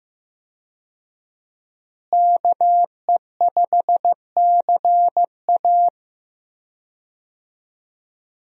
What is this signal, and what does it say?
Morse code, heard as a keyed tone, about 15 wpm, 700 Hz. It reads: KE5CA
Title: USA Call Signs - 5 Characters - Repeated 2x - 15wpm